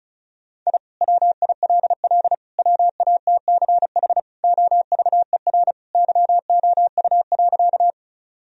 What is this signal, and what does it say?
Morse code, heard as a keyed tone, about 36 wpm, 700 Hz. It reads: I WILL WATCH OVER YOU.
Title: Sentences from Top 1000 Words - Review Part 1 - 35wpm